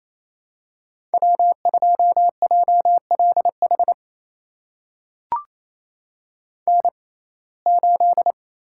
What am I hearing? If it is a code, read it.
Morse code, heard as a keyed tone, about 27 wpm, spaced as for 20 wpm, 700 Hz. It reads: W2JLH N8
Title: USA Call Signs - 5 Characters - 6x Character Spacing - 28wpm